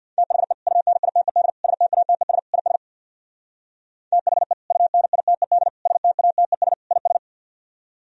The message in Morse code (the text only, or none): THE UNITED STATES IS THE UNITED STATES IS